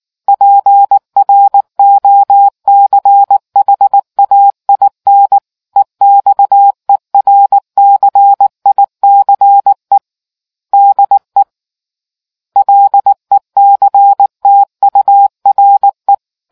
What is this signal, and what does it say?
Morse code, heard as a keyed tone, about 19 wpm, 800 Hz. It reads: PROCHAIN EXERCICE DE LECTURE